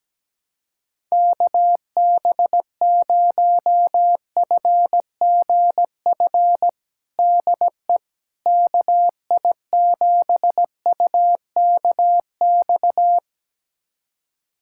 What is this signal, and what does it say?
Morse code, heard as a keyed tone, about 17 wpm, 700 Hz. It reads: KB0FGF DE KI7UKX
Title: USA Callsign DE Callsign - 6 Characters - 17wpm